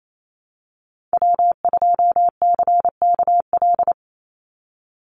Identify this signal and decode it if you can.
Morse code, heard as a keyed tone, about 28 wpm, 700 Hz. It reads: W2CKL
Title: NA SOTA Activator Callsigns 28wpm